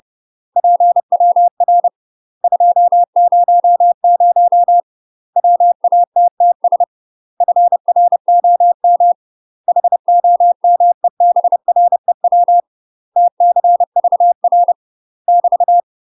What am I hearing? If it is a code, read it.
Morse code, heard as a keyed tone, about 30 wpm, 700 Hz. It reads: PWR 200 WATTS FROM HOMEBREW TCVR <BT>
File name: UK MASTER QSOs TRAINING / 30WPM / UK MASTER QSOs 0022